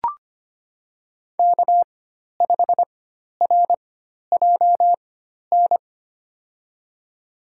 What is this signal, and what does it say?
Morse code, heard as a keyed tone, about 25 wpm, spaced as for 12 wpm, 690 Hz. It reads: K5RJN
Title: USA Call Signs - 4x Character Spacing - 25wpm